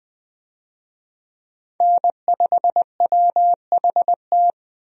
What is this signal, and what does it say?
Morse code, heard as a keyed tone, about 20 wpm, 700 Hz. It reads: N5WHT
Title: USA Call Signs - 5 Characters - Repeated 2x - 20wpm